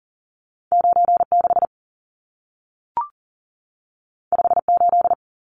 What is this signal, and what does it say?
Morse code, heard as a keyed tone, about 40 wpm, 700 Hz. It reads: 96 58
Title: Numbers - 2-Digit 40wpm